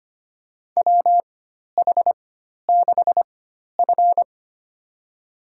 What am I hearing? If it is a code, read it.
Morse code, heard as a keyed tone, about 25 wpm, spaced as for 12 wpm, 700 Hz. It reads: WH6F
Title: USA Call Signs - 4 Characters - 4x Character Spacing - 25wpm